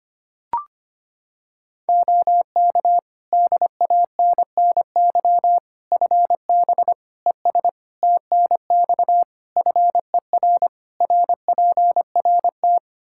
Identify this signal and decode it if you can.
Morse code, heard as a keyed tone, about 25 wpm, 700 Hz. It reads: OK DANNY FB ES TNX FER RPRT
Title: Practice QSO - Exchange 2 - Info - 25wpm